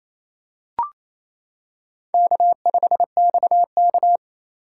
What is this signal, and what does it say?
Morse code, heard as a keyed tone, about 28 wpm, 700 Hz. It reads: K5XK
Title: K1USN SST QSO Practice - 28wpm